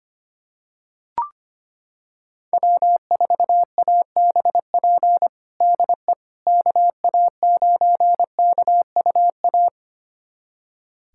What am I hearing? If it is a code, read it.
Morse code, heard as a keyed tone, about 25 wpm, 700 Hz. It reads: W4ABP DE KA9KUA
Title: USA Callsign DE Callsign 25wpm